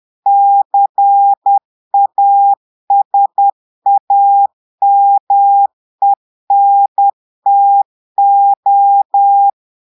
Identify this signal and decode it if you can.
Morse code, heard as a keyed tone, about 10 wpm, 800 Hz. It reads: CASAMENTO
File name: Portugais - texte en clair 1 - vitesse - 10 mots minute